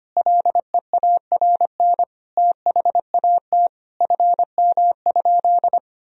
Morse code, heard as a keyed tone, about 25 wpm, 700 Hz. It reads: LEARN THAT FM?
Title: Sentences from Top 300 Words and Abbreviations 25wpm